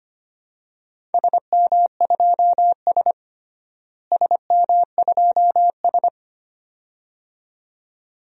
Morse code, heard as a keyed tone, about 25 wpm, 700 Hz. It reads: SM2S SM2S
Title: DXCC-100 Call Signs - 4 Characters - Repeated 2x - 25wpm